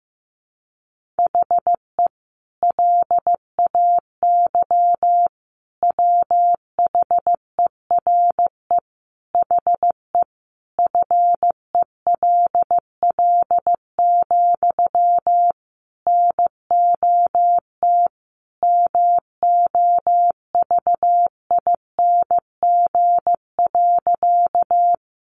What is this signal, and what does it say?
Morse code, heard as a keyed tone, about 15 wpm, 700 Hz. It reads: HE LAY WHERE HE FELL, NOT MOVING.
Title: A Sound Of Thunder 15wpm